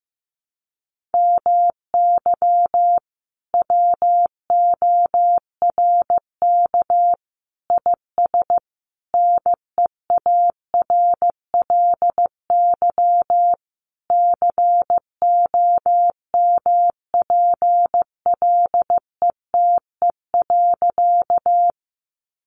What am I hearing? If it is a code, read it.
Morse code, heard as a keyed tone, about 15 wpm, 700 Hz. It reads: MY WORK IS NEARLY COMPLETE.
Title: Sentences from Top 1300 Words 15wpm